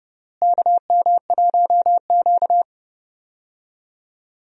Morse code, heard as a keyed tone, about 30 wpm, 700 Hz. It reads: KM1Q
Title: USA Call Signs - 4 Characters - Repeated 3x - 30wpm